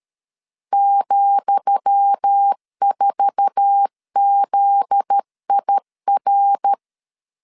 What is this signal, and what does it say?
Morse code, heard as a keyed tone, about 13 wpm, 800 Hz. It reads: ,4ZIR
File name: LEZIONI / 10 SIGNS / 10D COMA 12wpm 2544